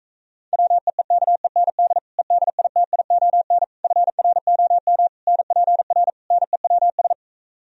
Morse code, heard as a keyed tone, about 38 wpm, 700 Hz. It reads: WEEKEND EDITION FROM NPR NEWS